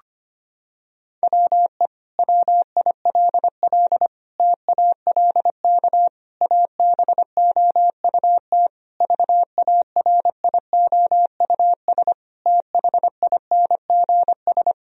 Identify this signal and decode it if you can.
Morse code, heard as a keyed tone, about 25 wpm, 700 Hz. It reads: WE WILL TALK ABOUT VARIOUS THINGS
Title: Sentences from Top 1000 Words - Review Part 4 - 25wpm